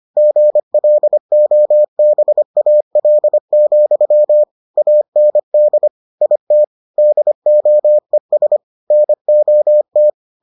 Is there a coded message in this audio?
Yes: Morse code from 0.2 to 10.1 s, heard as a keyed tone, about 25 wpm, 600 Hz. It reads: GLOBAL, AND IT DOES NOT